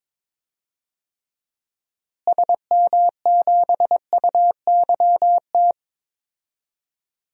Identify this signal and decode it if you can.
Morse code, heard as a keyed tone, about 22 wpm, 700 Hz. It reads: SM7UYT
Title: DXCC-100 Call Signs - 6 Characters - Repeated 2x - 22wpm